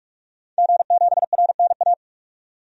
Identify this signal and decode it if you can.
Morse code, heard as a keyed tone, about 44 wpm, 700 Hz. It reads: G7RNA
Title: DXCC-100 Call Signs - 5 Characters - 45wpm